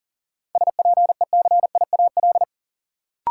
Morse code, heard as a keyed tone, about 40 wpm, 700 Hz. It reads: SPECIAL
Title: Sentences from Top 1000 Words - Review Part 3 - New Word Drill - 40wpm